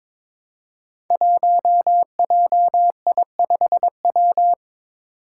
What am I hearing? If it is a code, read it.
Morse code, heard as a keyed tone, about 22 wpm, 700 Hz. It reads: 1JI5W
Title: Five Challenging Letter-Number Combinations 22wpm